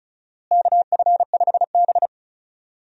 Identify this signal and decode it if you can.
Morse code, heard as a keyed tone, about 36 wpm, 700 Hz. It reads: KF5B
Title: USA Call Signs - 4 Characters - Repeated 2x - 35wpm